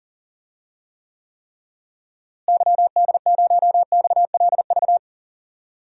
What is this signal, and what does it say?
Morse code, heard as a keyed tone, about 38 wpm, 700 Hz. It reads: YD0XLV
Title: DXCC-100 Call Signs - 6 Characters - Repeated 3x - 40wpm